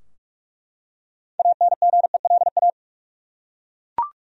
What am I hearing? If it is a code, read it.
Morse code, heard as a keyed tone, about 48 wpm, 700 Hz. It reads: ANGELA
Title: USA Common Names - 45wpm